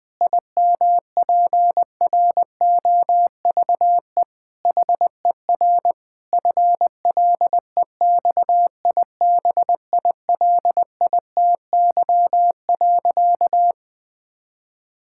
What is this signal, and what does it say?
Morse code, heard as a keyed tone, about 20 wpm, 700 Hz. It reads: IMPROVE HER FLEXIBILITY.
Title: Difficult Sentences with Hints - 20wpm